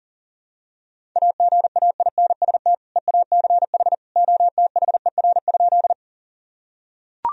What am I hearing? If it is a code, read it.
Morse code, heard as a keyed tone, about 40 wpm, 700 Hz. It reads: AGAINST EACH OTHER?
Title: Sentences from Top 1000 Words - Review Part 1 - 40wpm